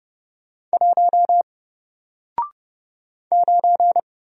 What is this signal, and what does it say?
Morse code, heard as a keyed tone, about 30 wpm, 700 Hz. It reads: 1 9
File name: Single-Letter-Number-30wpm